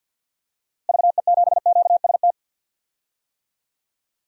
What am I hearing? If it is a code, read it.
Morse code, heard as a keyed tone, about 52 wpm, 700 Hz. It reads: VE7QST
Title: DXCC-100 Call Signs - 6 Characters - Repeated 3x - 50wpm